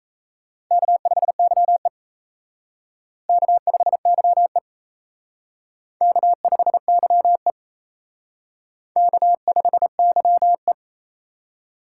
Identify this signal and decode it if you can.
Morse code, heard as a keyed tone, about 34 wpm, 700 Hz. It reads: K5YE K5YE K5YE K5YE
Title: Speed Racing - USA 1x2 Call Signs 28wpm